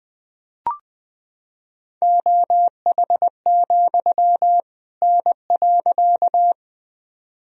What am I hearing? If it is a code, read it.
Morse code, heard as a keyed tone, about 21 wpm, 700 Hz. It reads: OH, N.
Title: Sentences from Top 500 Words and Abbreviations 20wpm